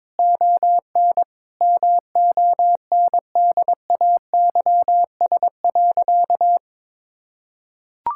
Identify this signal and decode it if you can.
Morse code, heard as a keyed tone, about 22 wpm, 700 Hz. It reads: ON MONDAYS.
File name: practice / 22wpm / Easy Sentences with Hints - 22wpm